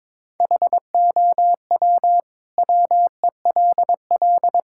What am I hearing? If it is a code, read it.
Morse code, heard as a keyed tone, about 22 wpm, 700 Hz. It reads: HOW WELL
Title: Sentences from Top 1000 Words - Review Part 6 - 22wpm